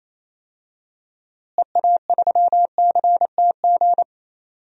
Morse code, heard as a keyed tone, about 28 wpm, 700 Hz. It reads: EA3CTG
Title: DXCC-100 Call Signs - 6 Characters - Repeated 3x - 28wpm